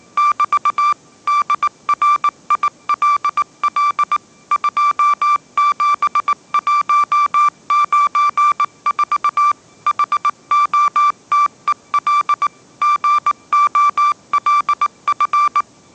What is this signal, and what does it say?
Morse code, heard as a keyed tone, about 19 wpm, 1170 Hz. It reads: <BT> DRILL 27194 HOTEL GOLF
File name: mrx083 20wpm